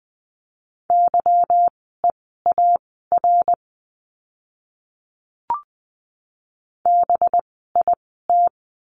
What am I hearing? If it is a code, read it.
Morse code, heard as a keyed tone, about 20 wpm, spaced as for 15 wpm, 700 Hz. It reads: YEAR BIT
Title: 185 Lesson 30f - Letter b - Words 20wpm